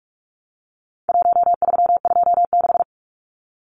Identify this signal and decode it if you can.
Morse code, heard as a keyed tone, about 44 wpm, 700 Hz. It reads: 1326
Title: Numbers - 4-Digit 45wpm